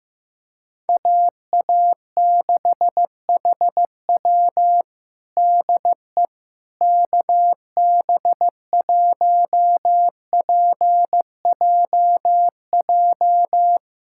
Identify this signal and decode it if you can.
Morse code, heard as a keyed tone, about 15 wpm, 700 Hz. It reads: AA6HW DE KB1PJJ
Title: USA Callsign DE Callsign 15wpm